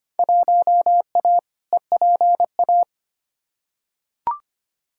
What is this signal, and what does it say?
Morse code, heard as a keyed tone, about 25 wpm, 700 Hz. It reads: 1A EPA
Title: Field Day Exchanges - 25wpm